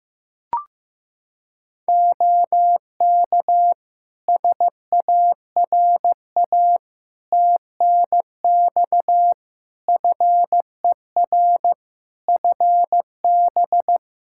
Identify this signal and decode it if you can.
Morse code, heard as a keyed tone, about 15 wpm, 700 Hz. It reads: OK SARA TNX FER FB
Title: Practice QSO - Exchange 4 - Ending - 15wpm